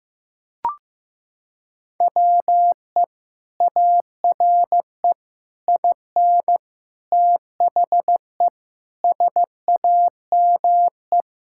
Morse code, heard as a keyed tone, about 15 wpm, 700 Hz. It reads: WE ARE IN THE SAME
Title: Sentences from Top 1000 Words - Review Part 6 - 15wpm